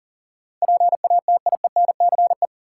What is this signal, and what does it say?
Morse code, heard as a keyed tone, about 40 wpm, 700 Hz. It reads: PATIENCE
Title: Sentences from Top 1300 Words - New Word Drill - 40wpm